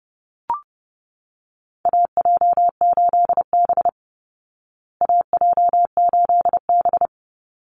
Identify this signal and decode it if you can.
Morse code, heard as a keyed tone, about 28 wpm, 700 Hz. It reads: AJ8B AJ8B
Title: Call Signs Repeated 3x - 30wpm